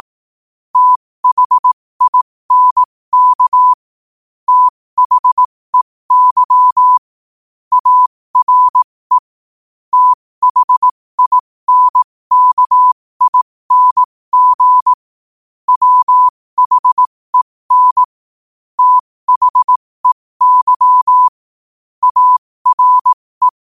Morse code, heard as a keyed tone, about 18 wpm, spaced as for 15 wpm, 1000 Hz. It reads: THINK THEY ARE THINKING WHEN THEY ARE